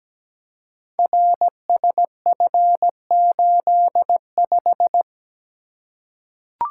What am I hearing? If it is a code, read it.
Morse code, heard as a keyed tone, about 17 wpm, 700 Hz. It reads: RSF85
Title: Code Groups - 5 Letter:Numbers - 17wpm